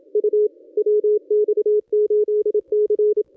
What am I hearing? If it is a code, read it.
Morse code, heard as a keyed tone, about 27 wpm, 440 Hz. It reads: U WX8C
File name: contests / ss-cw-k5tr-op-n5rz -1100